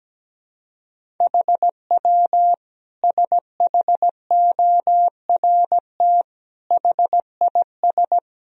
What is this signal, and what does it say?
Morse code, heard as a keyed tone, about 17 wpm, 700 Hz. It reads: HW SHORT HIS
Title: Sentences from Top 400 Words and Abbreviations 17wpm